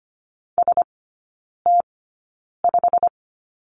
Morse code, heard as a keyed tone, about 25 wpm, spaced as for 9 wpm, 700 Hz. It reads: ST5
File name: Lesson 15e - Number 5 - Three Characters 25wpm